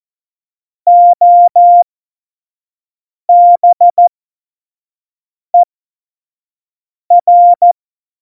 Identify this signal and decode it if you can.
Morse code, heard as a keyed tone, about 14 wpm, spaced as for 5 wpm, 700 Hz. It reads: OBER
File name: Morsekurs / Witze / Witz 17. September 2019, 25 F70